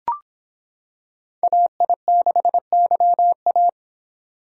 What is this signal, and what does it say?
Morse code, heard as a keyed tone, about 25 wpm, 700 Hz. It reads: AI6YA